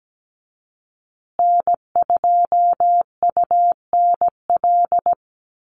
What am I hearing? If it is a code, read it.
Morse code, heard as a keyed tone, about 17 wpm, 700 Hz. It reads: N2UNL
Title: USA Call Signs Repeated 2x 17wpm